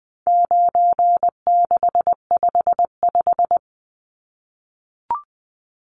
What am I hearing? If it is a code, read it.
Morse code, heard as a keyed tone, about 20 wpm, 700 Hz. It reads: 9655
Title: Numbers - 4-Digit 20wpm